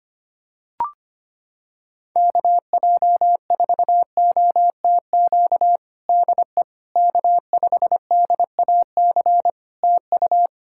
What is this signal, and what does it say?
Morse code, heard as a keyed tone, about 25 wpm, 700 Hz. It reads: KJ4OTQ DE K5DAC TU
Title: Practice QSO - Exchange 4 - Ending - 25wpm